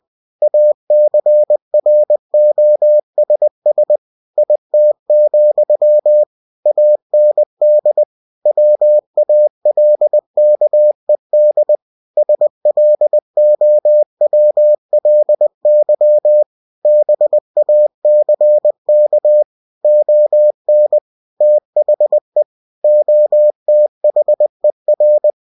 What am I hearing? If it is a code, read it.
Morse code, heard as a keyed tone, about 20 wpm, 600 Hz. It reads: ACROSS IT, AND WALKED SLOWLY BACK ON THE OTHER
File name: cw-books / winnie-the-pooh / winnie-20wpm-page-0044